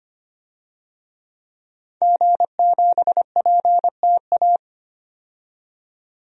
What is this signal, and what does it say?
Morse code, heard as a keyed tone, about 25 wpm, 700 Hz. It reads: G7PTA
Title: DXCC-100 Call Signs - 5 Characters - 25wpm